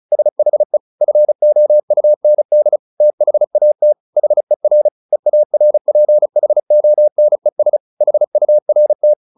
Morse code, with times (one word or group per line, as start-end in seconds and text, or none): 0.1-0.8 SHE
1.0-2.8 FOUND
3.0-3.9 THAT
4.2-4.9 HER
5.1-7.8 EARPHONES
8.0-9.1 HURT